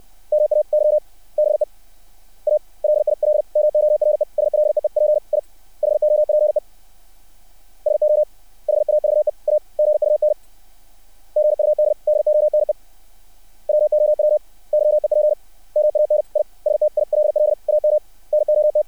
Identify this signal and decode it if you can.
Morse code, heard as a keyed tone, about 60 wpm, 600 Hz. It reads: MAYBE A LIL MORE SLEEP I HOPE SO HAVE A GUN GUD DONE WOW YEP MAN I STILL AM NOT
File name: archived / CRSnet